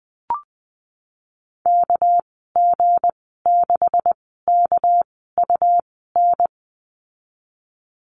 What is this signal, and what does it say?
Morse code, heard as a keyed tone, about 20 wpm, spaced as for 15 wpm, 700 Hz. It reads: KG6KUN
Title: 200 Lesson 32g - Letter k - Call Signs 20wpm